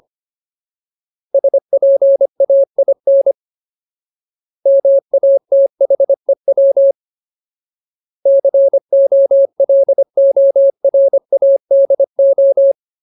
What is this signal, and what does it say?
Morse code, heard as a keyed tone, about 25 wpm, 550 Hz. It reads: SPAIN MATHEW COLORADO
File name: Ultimate Word List / 25WPM Ultimate